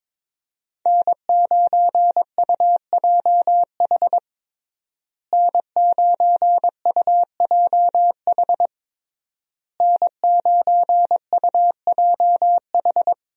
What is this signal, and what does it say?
Morse code, heard as a keyed tone, about 22 wpm, 700 Hz. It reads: N9UJH N9UJH N9UJH
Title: USA Call Signs - 5 Characters - Repeated 3x - 22wpm